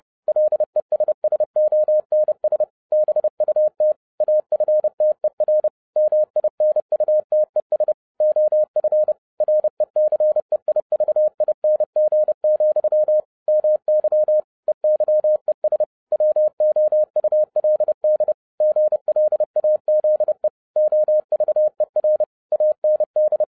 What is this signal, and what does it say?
Morse code, heard as a keyed tone, about 30 wpm, 610 Hz. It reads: LESSONS BUT AFTER MINUTES OF RECEIVING, MY EYES WOULD GLAZE OVER AND